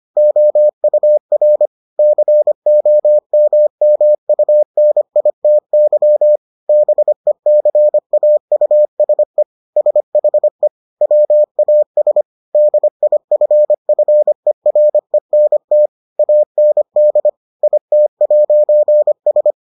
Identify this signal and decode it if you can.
Morse code, heard as a keyed tone, about 25 wpm, 600 Hz. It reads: OUR COMMUNITY BECAUSE SHE WAS DIFFERENT AND IT'S